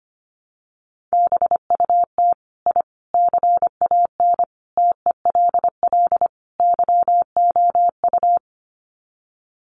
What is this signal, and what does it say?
Morse code, heard as a keyed tone, about 25 wpm, 700 Hz. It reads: BUT I CAN TELL YOU
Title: Sets of 5 Words Encore 25wpm